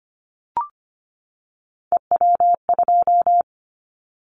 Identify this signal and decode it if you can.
Morse code, heard as a keyed tone, about 25 wpm, 700 Hz. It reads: EW2
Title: Callsign Prefixes for Top 100 DXCC - 25wpm